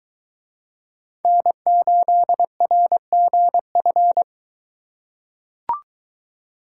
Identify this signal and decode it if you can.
Morse code, heard as a keyed tone, about 22 wpm, 700 Hz. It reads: N8RGF